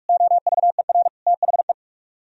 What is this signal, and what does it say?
Morse code, heard as a keyed tone, about 58 wpm, 700 Hz. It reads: OVER THE